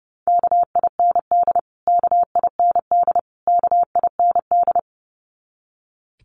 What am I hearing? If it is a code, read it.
Morse code, heard as a keyed tone, about 30 wpm, 700 Hz. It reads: KIND KIND KIND
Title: Top 200 Words Repeated 30wpm